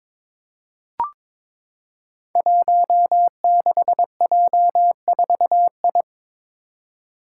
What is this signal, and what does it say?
Morse code, heard as a keyed tone, about 23 wpm, 700 Hz. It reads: 16J4I
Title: Five Challenging Letter-Number Combinations 22wpm